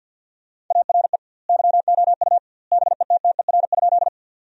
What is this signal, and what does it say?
Morse code, heard as a keyed tone, about 50 wpm, 700 Hz. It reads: ARE YOU BETTER?
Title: 2x Word-Spacing - Sentences from Top 400 Words - 50wpm